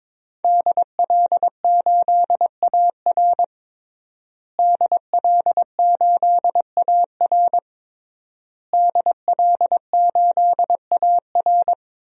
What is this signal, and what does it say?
Morse code, heard as a keyed tone, about 22 wpm, 700 Hz. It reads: DL8AR DL8AR DL8AR
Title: DXCC-100 Call Signs - 5 Characters - Repeated 3x - 22wpm